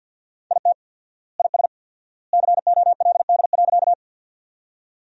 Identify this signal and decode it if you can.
Morse code, heard as a keyed tone, about 50 wpm, 700 Hz. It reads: IT IS COLD.
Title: 4x Word-Spacing - Sentences from Top 400 Words - 50wpm